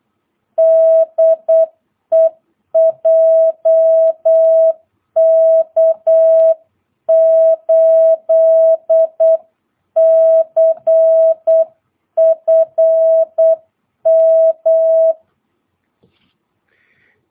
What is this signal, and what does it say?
Morse code, heard as a keyed tone, about 8 wpm, 650 Hz. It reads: DE JK8CFM
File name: jh8cbh / 2022cw